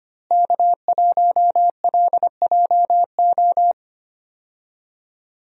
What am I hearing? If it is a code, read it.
Morse code, heard as a keyed tone, about 25 wpm, 700 Hz. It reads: K1LJO
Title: USA Call Signs - 5 Characters - Repeated 2x - 25wpm